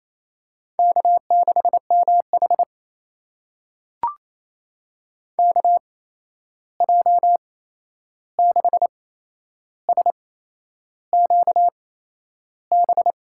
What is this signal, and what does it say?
Morse code, heard as a keyed tone, about 28 wpm, spaced as for 11 wpm, 700 Hz. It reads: K6MH KJ6SQB
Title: USA Call Signs - 8x Character Spacing - 28wpm